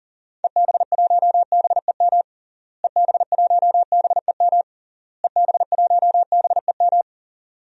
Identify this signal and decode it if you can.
Morse code, heard as a keyed tone, about 40 wpm, 700 Hz. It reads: EB1BEM EB1BEM EB1BEM
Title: DXCC-100 Call Signs - 6 Characters - Repeated 3x - 40wpm